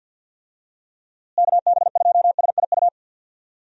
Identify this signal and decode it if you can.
Morse code, heard as a keyed tone, about 51 wpm, 700 Hz. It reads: KB2SIU
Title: USA Call Signs - 6 Characters - Repeated 2x - 50wpm